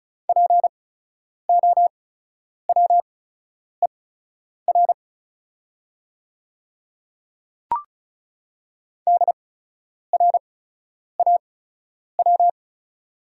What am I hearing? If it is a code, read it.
Morse code, heard as a keyed tone, about 34 wpm, spaced as for 10 wpm, 690 Hz. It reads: POWER DRAW
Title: 8x Character Spacing - Top 500 Words - 35wpm